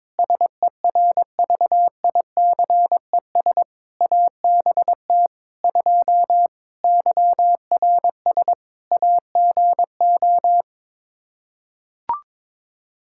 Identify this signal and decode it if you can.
Morse code, heard as a keyed tone, about 22 wpm, 700 Hz. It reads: SERVICES ABT 2 YRS AGO
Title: Practice QSO - Exchange 3 - More Info - 22wpm